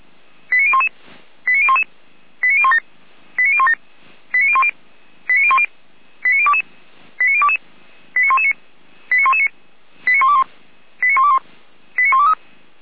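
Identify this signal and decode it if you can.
Five-tone selective call (ZVEI), heard as quick runs of five tones, tones of about 70 ms. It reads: 89011 89011 89017 89017 89010 89010 89022 89022 89119 89119 89121 89121 89123
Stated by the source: ZVEI five-tone selective call received by SDR.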